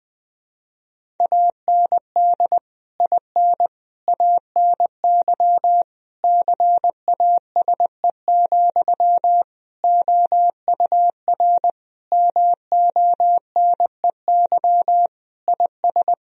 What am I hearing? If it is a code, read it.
Morse code, heard as a keyed tone, about 20 wpm, 700 Hz. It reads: AND IN ANY CASE, OUR MONEY IS
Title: Sentences from Top 1000 Words - Review Part 1 - 20wpm